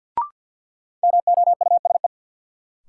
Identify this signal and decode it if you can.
Morse code, heard as a keyed tone, about 53 wpm, 700 Hz. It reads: MOUSE
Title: Sentences from Top 700 Words - New Word Drill - 50wpm